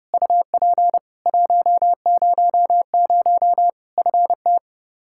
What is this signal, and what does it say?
Morse code, heard as a keyed tone, about 30 wpm, 700 Hz. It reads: UP 100 FT
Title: Practice QSO - Exchange 2 - Info - 30wpm